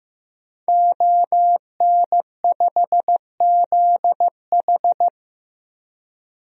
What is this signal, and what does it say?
Morse code, heard as a keyed tone, about 15 wpm, 700 Hz. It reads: ON5ZH
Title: DXCC-100 Call Signs - 5 Characters - 15wpm